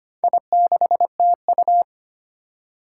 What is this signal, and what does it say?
Morse code, heard as a keyed tone, about 25 wpm, 700 Hz. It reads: I6TU
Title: Four Challenging Letter-Number Combinations 25wpm